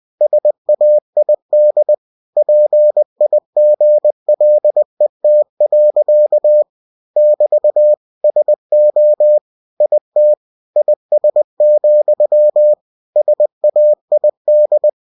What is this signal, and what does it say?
Morse code, heard as a keyed tone, about 20 wpm, 600 Hz. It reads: SAID PIGLET. <BT> SO IT IS, SAID
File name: cw-books / winnie-the-pooh / winnie-20wpm-page-0069